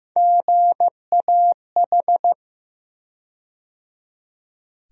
Morse code, heard as a keyed tone, about 15 wpm, 700 Hz. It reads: GAH
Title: USA Call Sign Postfixes - 3 Letters 15wpm